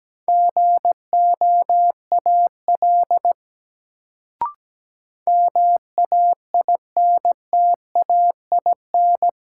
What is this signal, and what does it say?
Morse code, heard as a keyed tone, about 17 wpm, 700 Hz. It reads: GOAL MAINTAIN
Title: Sentences from Top 1000 Words - Review Part 5 - New Word Drill - 17wpm